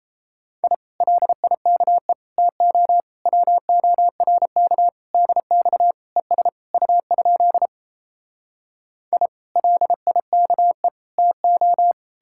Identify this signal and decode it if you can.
Morse code, heard as a keyed tone, about 37 wpm, 700 Hz. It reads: I LIKE TO WORK DX ES U? I LIKE TO